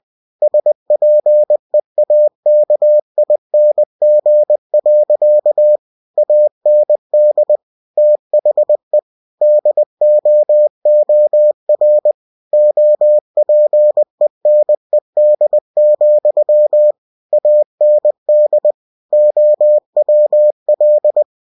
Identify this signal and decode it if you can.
Morse code, heard as a keyed tone, about 20 wpm, 600 Hz. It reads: SPEAKING. AND THE DOOR OPENED, AND OWL
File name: cw-books / winnie-the-pooh / winnie-20wpm-page-0029